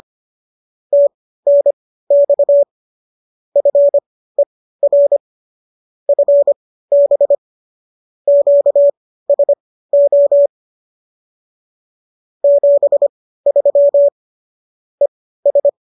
Morse code, heard as a keyed tone, about 25 wpm, spaced as for 15 wpm, 570 Hz. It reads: TNX FER FB QSO 73 ES